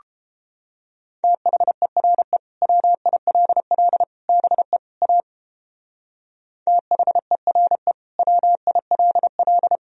Morse code, heard as a keyed tone, about 29 wpm, 700 Hz. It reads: THERE WILL BE A THERE WILL